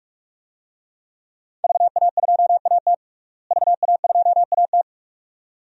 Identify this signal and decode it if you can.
Morse code, heard as a keyed tone, about 46 wpm, 700 Hz. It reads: VA2AT VA2AT
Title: DXCC-100 Call Signs - 5 Characters - Repeated 2x - 45wpm